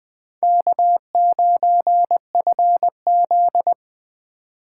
Morse code, heard as a keyed tone, about 20 wpm, 700 Hz. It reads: K9FZ
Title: Speed Racing - USA 1x2 Call Signs 17wpm